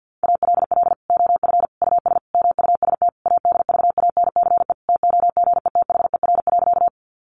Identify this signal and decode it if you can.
Morse code, heard as a keyed tone, about 50 wpm, 700 Hz. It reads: ALL OF US MUST ADVANCE TOGETHER.
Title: Sentences from Top 1400 Words 50wpm